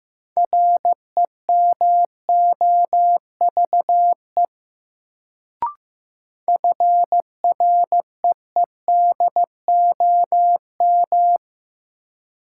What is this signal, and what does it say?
Morse code, heard as a keyed tone, about 15 wpm, 700 Hz. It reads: REMOVE FREEDOM
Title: Sentences from Top 800 Words - New Word Drill - 15wpm